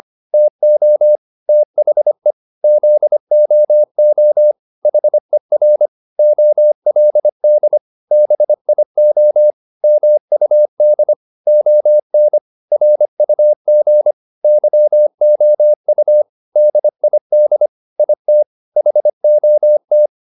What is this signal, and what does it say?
Morse code, heard as a keyed tone, about 25 wpm, 600 Hz. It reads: TO THE ZOO HER OLD BIO MUD ON RUG YOU DID IT HOT